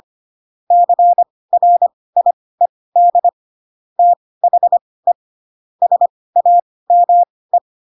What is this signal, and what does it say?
Morse code, heard as a keyed tone, about 25 wpm, spaced as for 18 wpm, 700 Hz. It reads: CRIED THE SAME
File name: Pinocchio / Pinocchio ch1 12 25 ik1hge